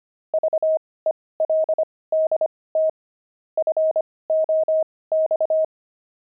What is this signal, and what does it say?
Morse code, heard as a keyed tone, about 25 wpm, spaced as for 18 wpm, 630 Hz. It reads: VELDT FOX
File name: pangram-fs-18-cs-25-veldt